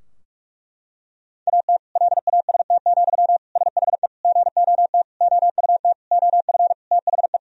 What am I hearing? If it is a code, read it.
Morse code, heard as a keyed tone, about 44 wpm, 700 Hz. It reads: AT LAST, SHE GOT OUT OF THE
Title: Sentences from Top 1000 Words - Review Part 2 - 45wpm